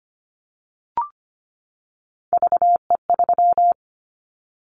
Morse code, heard as a keyed tone, about 25 wpm, 700 Hz. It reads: VE3
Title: Callsign Prefixes for Top 100 DXCC - 25wpm